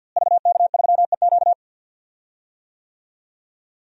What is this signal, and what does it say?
Morse code, heard as a keyed tone, about 51 wpm, 700 Hz. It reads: VK3EQ
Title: DXCC-100 Call Signs - 5 Characters - Repeated 3x - 50wpm